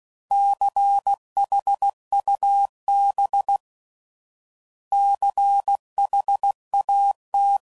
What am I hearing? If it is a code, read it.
Morse code, heard as a keyed tone, about 16 wpm, 800 Hz. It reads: CHUB CHAT
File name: Prefixes / By First Letter / Cs